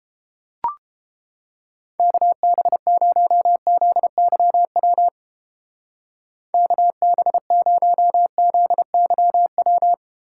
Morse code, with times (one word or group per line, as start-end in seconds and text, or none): 2.0-5.1 KB0ZYW
6.5-9.9 KB0ZYW